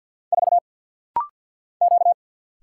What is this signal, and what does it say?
Morse code, heard as a keyed tone, about 48 wpm, 700 Hz. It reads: 4 Q
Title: Single Letter-Number - Rapid-Fire 50wpm